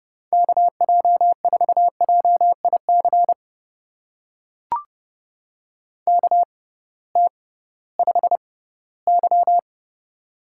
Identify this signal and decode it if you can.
Morse code, heard as a keyed tone, about 30 wpm, spaced as for 17 wpm, 700 Hz. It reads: KJ4JIC KT5Y
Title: USA Call Signs - 6x Character Spacing - 30wpm